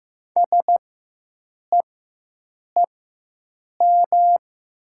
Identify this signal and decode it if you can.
Morse code, heard as a keyed tone, about 15 wpm, spaced as for 7 wpm, 700 Hz. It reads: SEEM
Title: 4x Character Spacing - Top 300 Words - 15wpm